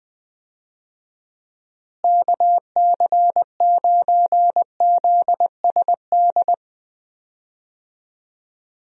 Morse code, heard as a keyed tone, about 20 wpm, 700 Hz. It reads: KC9ZSD
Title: USA Call Signs - 6 Characters - Repeated 3x - 20wpm